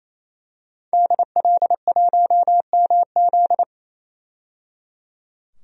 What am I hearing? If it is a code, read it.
Morse code, heard as a keyed tone, about 28 wpm, 700 Hz. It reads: DL1MZ
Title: DXCC-100 Call Signs - 5 Characters - 28wpm